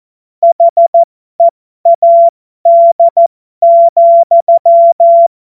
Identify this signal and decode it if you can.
Morse code, heard as a keyed tone, about 14 wpm, spaced as for 12 wpm, 680 Hz. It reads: HEAD,
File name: Morsekurs / Witze